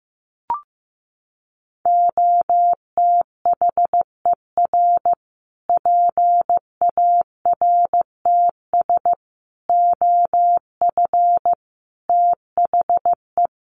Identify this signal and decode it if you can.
Morse code, heard as a keyed tone, about 15 wpm, 700 Hz. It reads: OTHER PARTS OF THE
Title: Sets of 5 Words Encore 15wpm